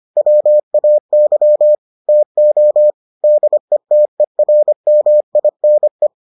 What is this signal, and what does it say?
Morse code, heard as a keyed tone, about 25 wpm, 600 Hz. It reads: WAY TO DETERMINE